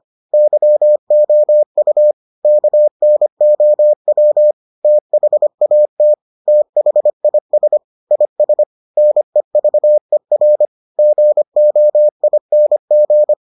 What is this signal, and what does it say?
Morse code, heard as a keyed tone, about 25 wpm, 600 Hz. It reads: YOU KNOW THAT THIS IS NEVER GOING